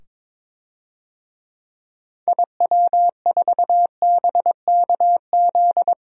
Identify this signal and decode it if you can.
Morse code, heard as a keyed tone, about 22 wpm, 700 Hz. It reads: IW4BKZ
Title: DXCC-100 Call Signs - 6 Characters - 22wpm